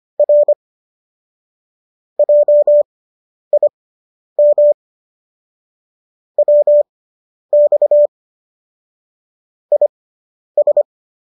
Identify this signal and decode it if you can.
Morse code, heard as a keyed tone, about 25 wpm, spaced as for 10 wpm, 590 Hz. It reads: R JIM WX IS